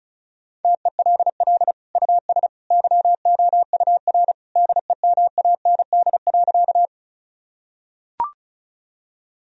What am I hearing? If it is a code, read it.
Morse code, heard as a keyed tone, about 36 wpm, 700 Hz. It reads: TELL US YOUR DEMAND.
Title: Sentences from Top 1000 Words - Review Part 4 - 35wpm